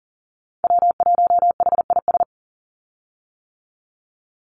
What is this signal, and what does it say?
Morse code, heard as a keyed tone, about 40 wpm, 700 Hz. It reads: W1HIS
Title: Call Signs - 40wpm